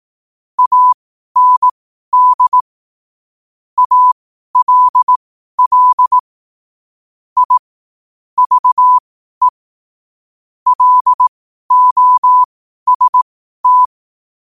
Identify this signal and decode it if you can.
Morse code, heard as a keyed tone, about 18 wpm, spaced as for 12 wpm, 1000 Hz. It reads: AND ALL I VE LOST
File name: qotdfiles / 12wpm